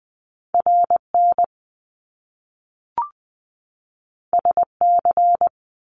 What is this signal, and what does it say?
Morse code, heard as a keyed tone, about 20 wpm, 700 Hz. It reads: RN SC
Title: Top Two Letter Combinations in Words 20wpm